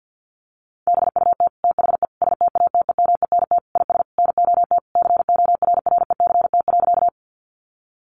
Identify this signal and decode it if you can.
Morse code, heard as a keyed tone, about 50 wpm, 700 Hz. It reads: BUT THE STATEMENT IS NOT CORRECT.
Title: Sentences from Top 1200 Words 50wpm